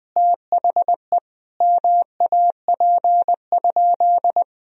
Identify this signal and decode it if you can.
Morse code, heard as a keyed tone, about 20 wpm, 700 Hz. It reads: THE MAP?
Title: Sentences from Top 1000 Words - Review Part 2 - 20wpm